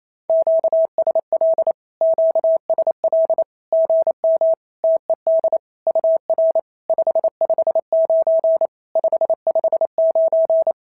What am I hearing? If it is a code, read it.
Morse code, heard as a keyed tone, about 28 wpm, 650 Hz. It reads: QSL QSL GM TED UR 559 559
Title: SOTA QSO Practice 28wpm